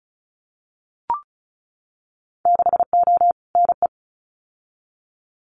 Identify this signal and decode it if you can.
Morse code, heard as a keyed tone, about 34 wpm, 700 Hz. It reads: 6O NE
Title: Winter Field Day Exchanges Encore 35wpm